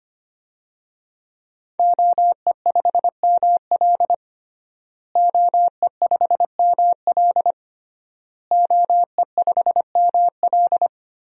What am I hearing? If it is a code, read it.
Morse code, heard as a keyed tone, about 25 wpm, 700 Hz. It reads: OE5ML OE5ML OE5ML
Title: DXCC-100 Call Signs - 5 Characters - Repeated 3x - 25wpm